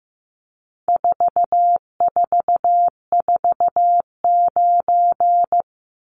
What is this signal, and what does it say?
Morse code, heard as a keyed tone, about 15 wpm, 700 Hz. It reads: 4449
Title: Numbers - 4-Digit 15wpm